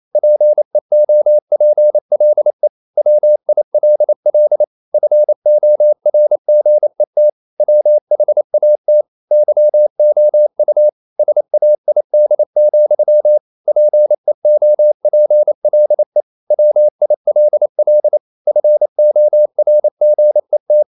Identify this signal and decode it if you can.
Morse code, heard as a keyed tone, about 28 wpm, 600 Hz. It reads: PEOPLE WILL FORGET WHAT YOU SAID, PEOPLE WILL FORGET
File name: CW / Practice / EnglishQuotations0228